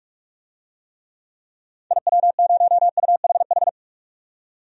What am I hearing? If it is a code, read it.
Morse code, heard as a keyed tone, about 46 wpm, 700 Hz. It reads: IW0UHH
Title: DXCC-100 Call Signs - 6 Characters - 45wpm